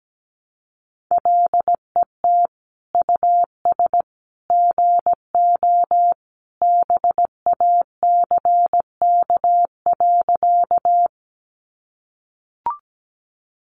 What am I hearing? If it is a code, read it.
Morse code, heard as a keyed tone, about 17 wpm, 700 Hz. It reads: LET US GO BACK.
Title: Sentences from Top 300 Words 17wpm